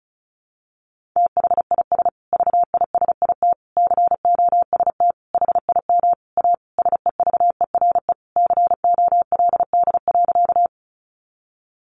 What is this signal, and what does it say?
Morse code, heard as a keyed tone, about 34 wpm, 700 Hz. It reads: THIS VISIT COST HIM A SEVERE COLD.
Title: Sentences from Top 1400 Words 35wpm